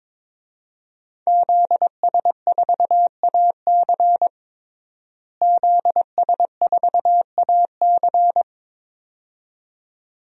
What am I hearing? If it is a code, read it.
Morse code, heard as a keyed tone, about 22 wpm, 700 Hz. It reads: ZS4AC ZS4AC
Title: DXCC-100 Call Signs - 5 Characters - Repeated 2x - 22wpm